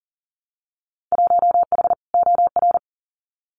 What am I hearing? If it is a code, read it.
Morse code, heard as a keyed tone, about 40 wpm, 700 Hz. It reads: 1H OR
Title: Winter Field Day Exchanges 40wpm